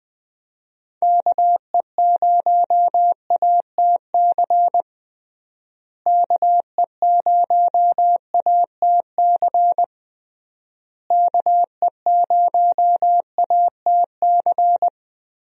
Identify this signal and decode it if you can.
Morse code, heard as a keyed tone, about 20 wpm, 700 Hz. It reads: KE0ATC KE0ATC KE0ATC
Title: USA Call Signs - 6 Characters - Repeated 3x - 20wpm